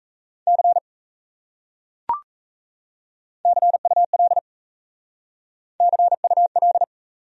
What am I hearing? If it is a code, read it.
Morse code, heard as a keyed tone, about 35 wpm, 700 Hz. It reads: C CUL CUL
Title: Speed-Racing Abbreviations - 28wpm